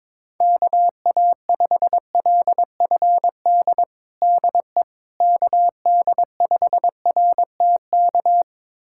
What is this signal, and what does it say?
Morse code, heard as a keyed tone, about 22 wpm, 700 Hz. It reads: KA5LFD DE KD5RTK
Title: USA Callsign DE Callsign - 6 Characters - 22wpm